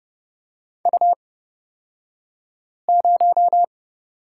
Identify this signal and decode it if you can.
Morse code, heard as a keyed tone, about 30 wpm, spaced as for 5 wpm, 700 Hz. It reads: U0
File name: Lesson 36d - BT - Two Characters 30wpm